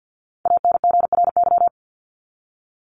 Morse code, heard as a keyed tone, about 50 wpm, 700 Hz. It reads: ANGRY
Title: Top 1751-2000 Words 50wpm